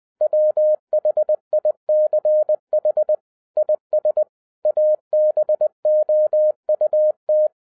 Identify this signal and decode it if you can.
Morse code, heard as a keyed tone, about 20 wpm, 610 Hz. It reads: WHICH IS ABOUT
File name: 2011-10-mp3 / n03-TARO JR0QWW 20wpm